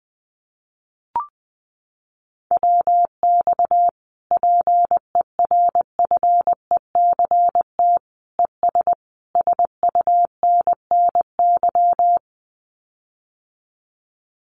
Morse code, heard as a keyed tone, about 20 wpm, 700 Hz. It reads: WX PERFECT ES SUNNY
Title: Common QSO Phrases Encore 20wpm